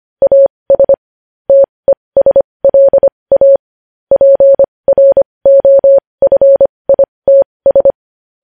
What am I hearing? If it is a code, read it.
Morse code, heard as a keyed tone, about 25 wpm, spaced as for 20 wpm, 550 Hz. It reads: AS TESLA PROFITS
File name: podcast / News headlines-1x-25x20